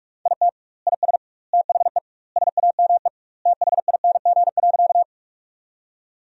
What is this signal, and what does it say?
Morse code, heard as a keyed tone, about 42 wpm, 700 Hz. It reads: IT IS THE SAME THING.
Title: 2x Word-Spacing - Sentences from Top 200 Words - 45wpm